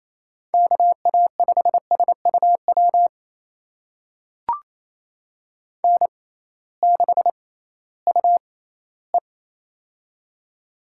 Morse code, heard as a keyed tone, about 28 wpm, spaced as for 16 wpm, 700 Hz. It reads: KA5SUW N6UE
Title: USA Call Signs - 6x Character Spacing - 28wpm